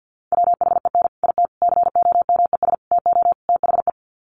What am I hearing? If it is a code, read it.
Morse code, heard as a keyed tone, about 50 wpm, 700 Hz. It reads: WHEN IT COMES TO THE
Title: Sets of 5 Words 50wpm